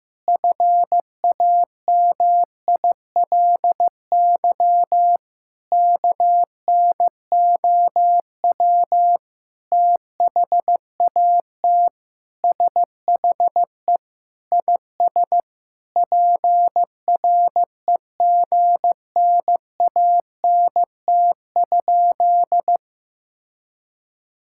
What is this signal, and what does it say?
Morse code, heard as a keyed tone, about 15 wpm, 700 Hz. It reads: FAMILY KNOW THAT SHE IS PREGNANT?
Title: Sentences from Top 1000 Words - Review Part 6 - 15wpm